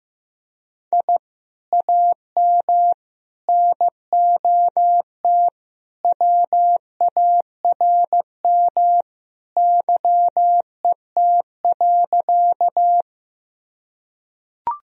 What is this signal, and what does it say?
Morse code, heard as a keyed tone, about 15 wpm, 700 Hz. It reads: I AM NOT WARM YET.
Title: Sentences from Top 1000 Words - Review Part 2 - 15wpm